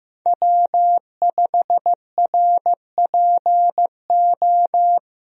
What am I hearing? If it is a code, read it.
Morse code, heard as a keyed tone, about 16 wpm, 700 Hz. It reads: W5RPO
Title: USA Call Signs - 5 Characters - 15wpm